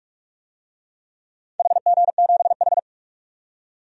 Morse code, heard as a keyed tone, about 46 wpm, 700 Hz. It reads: HG7H
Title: DXCC-100 Call Signs - 4 Characters - 45wpm